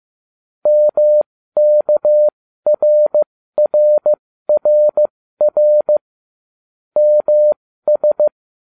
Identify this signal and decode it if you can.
Morse code, heard as a keyed tone, about 15 wpm, spaced as for 13 wpm, 600 Hz. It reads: MKRRRR MS